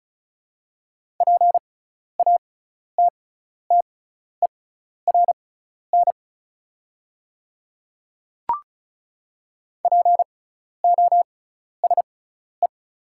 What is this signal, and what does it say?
Morse code, heard as a keyed tone, about 36 wpm, spaced as for 12 wpm, 690 Hz. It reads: PATTERN POSE
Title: 6x Character Spacing - Top 500 Words - 35wpm